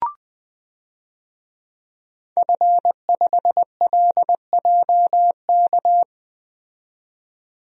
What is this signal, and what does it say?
Morse code, heard as a keyed tone, about 20 wpm, 700 Hz. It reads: F5LJK
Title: DXCC-100 Call Signs - 5 Characters - 20wpm